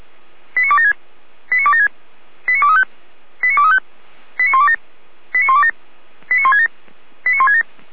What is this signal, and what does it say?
Five-tone selective call (ZVEI), heard as quick runs of five tones, tones of about 70 ms. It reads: 89276 89276 89235 89235 89127 89127 89167 89167